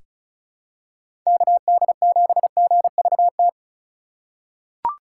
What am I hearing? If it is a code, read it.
Morse code, heard as a keyed tone, about 36 wpm, 700 Hz. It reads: KD7GVT
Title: USA Call Signs - 4x Character Spacing - 35wpm